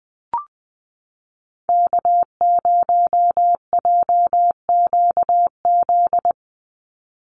Morse code, heard as a keyed tone, about 20 wpm, 700 Hz. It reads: K0JQZ
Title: NA SOTA Activator Callsigns 20wpm